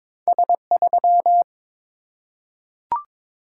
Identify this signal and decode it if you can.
Morse code, heard as a keyed tone, about 22 wpm, 700 Hz. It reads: S3
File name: practice / 22wpm / Two Letter-Number Combinations - 22wpm